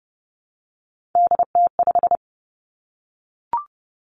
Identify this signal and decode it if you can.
Morse code, heard as a keyed tone, about 30 wpm, 700 Hz. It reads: DT5
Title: Three Challenging Letter-Number Combinations 30wpm